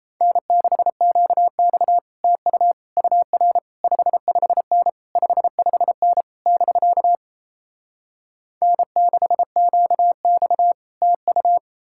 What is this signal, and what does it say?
Morse code, heard as a keyed tone, about 38 wpm, 700 Hz. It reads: N6QX TU UR 55N 55N <BK> N6QX TU